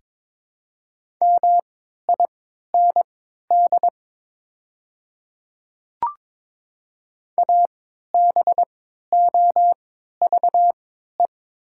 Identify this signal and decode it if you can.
Morse code, heard as a keyed tone, about 22 wpm, spaced as for 13 wpm, 700 Hz. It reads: MIND ABOVE
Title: 3x Character Spacing - Top 500 Words - 22wpm